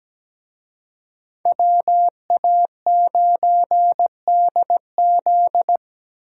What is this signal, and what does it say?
Morse code, heard as a keyed tone, about 19 wpm, 700 Hz. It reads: WA9DZ
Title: Speed Racing - USA 2x2 Call Signs 17wpm